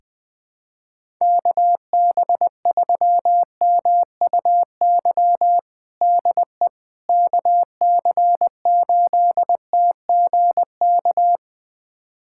Morse code, heard as a keyed tone, about 21 wpm, 700 Hz. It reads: KB3MUY DE KC8TGK
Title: USA Callsign DE Callsign - 6 Characters - 20wpm